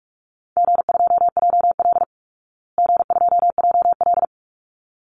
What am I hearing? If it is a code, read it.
Morse code, heard as a keyed tone, about 44 wpm, 700 Hz. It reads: G2JL G2JL
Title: Easy Call Signs Repeated 2x - 45wpm